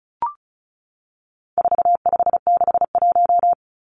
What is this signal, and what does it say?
Morse code, heard as a keyed tone, about 34 wpm, 700 Hz. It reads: 4561
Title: Numbers - 4-Digit 35wpm